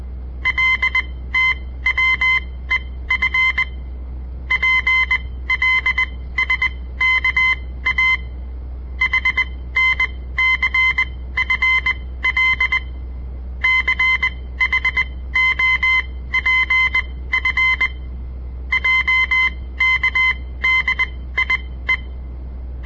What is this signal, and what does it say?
Morse code, heard as a keyed tone, about 20 wpm, spaced as for 15 wpm, 2050 Hz. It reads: LTWEF PLSKA HNCFL CHOPF JKDIE
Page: On the air Morse code practice from the Southern Montana Amatuer Radio Association; all code, no read back